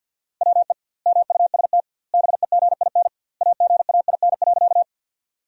Morse code, heard as a keyed tone, about 47 wpm, 700 Hz. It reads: WE MUST BEGIN AGAIN.
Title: 2x Word-Spacing - Sentences from Top 300 Words - 50wpm